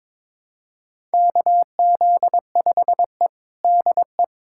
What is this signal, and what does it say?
Morse code, heard as a keyed tone, about 21 wpm, 700 Hz. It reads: KZ5E DE
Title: USA Callsign DE Callsign - 4 Characters - 22wpm